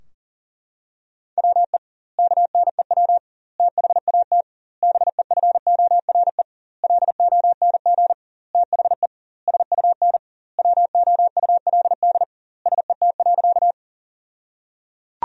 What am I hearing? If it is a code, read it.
Morse code, heard as a keyed tone, about 37 wpm, 700 Hz. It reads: WE KNEW THAT BEFORE LONG THE SUN WOULD SET.
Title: 2x Word-Spacing - Sentences from Top 400 Words - 40wpm